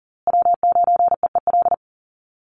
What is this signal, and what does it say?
Morse code, heard as a keyed tone, about 40 wpm, 700 Hz. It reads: W9EEL
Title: USA Call Signs Repeated 2x 40wpm